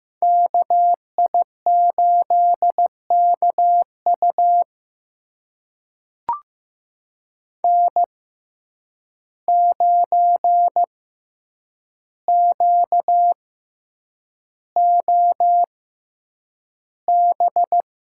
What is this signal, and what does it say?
Morse code, heard as a keyed tone, about 15 wpm, spaced as for 8 wpm, 700 Hz. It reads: KI8KU N9QOB
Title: USA Call Signs - 5 Characters - 6x Character Spacing - 15wpm